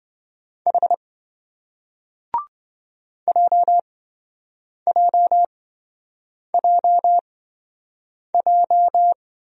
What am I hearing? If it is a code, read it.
Morse code, heard as a keyed tone, about 27 wpm, spaced as for 7 wpm, 700 Hz. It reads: H JJJJ